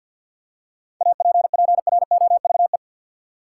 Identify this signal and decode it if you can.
Morse code, heard as a keyed tone, about 50 wpm, 700 Hz. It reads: APPROVE
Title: Seven Letter Words - 50wpm